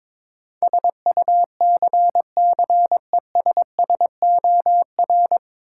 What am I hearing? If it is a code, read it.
Morse code, heard as a keyed tone, about 22 wpm, 700 Hz. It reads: SUCCESSOR
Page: Morse Code Ninja practice 22wpm